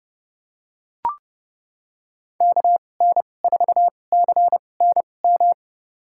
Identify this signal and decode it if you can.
Morse code, heard as a keyed tone, about 30 wpm, spaced as for 22 wpm, 700 Hz. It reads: KN4CNM
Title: USA Call Signs - 6 Characters - 2x Character Spacing - 30wpm